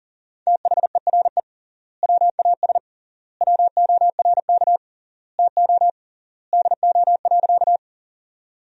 Morse code, heard as a keyed tone, about 40 wpm, 700 Hz. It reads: THERE WAS WORK TO DO.
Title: 3x Word-Spacing - Sentences from Top 200 Words - 40wpm